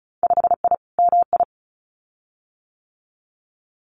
Morse code, heard as a keyed tone, about 36 wpm, 700 Hz. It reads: 5I MI
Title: Winter Field Day Exchanges Encore 35wpm